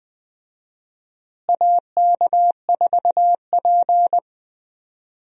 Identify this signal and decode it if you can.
Morse code, heard as a keyed tone, about 20 wpm, 700 Hz. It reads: AK4P
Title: USA Call Signs - 4 Characters - Repeated 2x - 20wpm